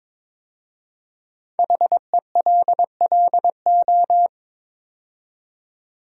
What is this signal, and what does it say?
Morse code, heard as a keyed tone, about 22 wpm, 700 Hz. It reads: HELLO
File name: 5 Letter Words - First 3 Letters Given - 22wpm